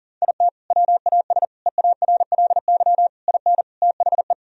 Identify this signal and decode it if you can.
Morse code, heard as a keyed tone, about 40 wpm, 700 Hz. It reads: IT WAS EARLY IN THE
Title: Sentences from Top 1000 Words - Review Part 2 - 40wpm